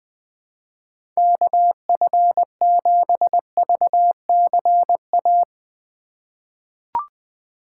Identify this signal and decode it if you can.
Morse code, heard as a keyed tone, about 20 wpm, 700 Hz. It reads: KF7VCA
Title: USA Call Signs - 6 Characters - 4x Character Spacing - 20wpm